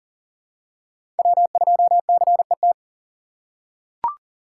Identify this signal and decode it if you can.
Morse code, heard as a keyed tone, about 40 wpm, 700 Hz. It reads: W2CET
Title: USA Call Signs - 5 Characters - 3x Character Spacing - 40wpm